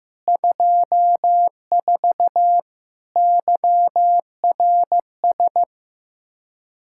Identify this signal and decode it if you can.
Morse code, heard as a keyed tone, about 15 wpm, 700 Hz. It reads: 24 YRS
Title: Practice QSO - Exchange 3 - More Info - 15wpm